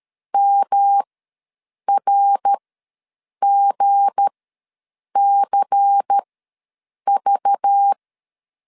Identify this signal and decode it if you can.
Morse code, heard as a keyed tone, about 13 wpm, spaced as for 7 wpm, 800 Hz. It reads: MRGCV
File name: LEZIONI / 8 YJ45 / 8 YJ ws 12wpm 3342